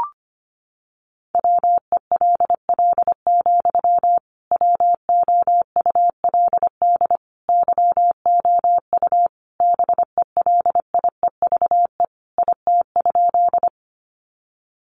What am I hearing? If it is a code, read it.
Morse code, heard as a keyed tone, about 25 wpm, 700 Hz. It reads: WELL, WOULD YOU BELIEVE IT?
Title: Sentences from Top 1000 Words 25wpm